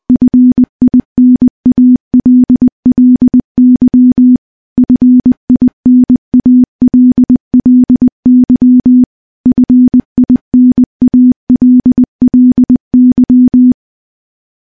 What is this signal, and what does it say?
Morse code, heard as a keyed tone, about 21 wpm, 250 Hz. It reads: FINALLY FINALLY FINALLY
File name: morse / morse-training-20WPM